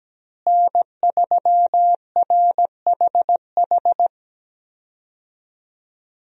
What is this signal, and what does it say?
Morse code, heard as a keyed tone, about 17 wpm, 700 Hz. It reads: N3RHH
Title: USA Call Signs - 5 Characters - Repeated 2x - 17wpm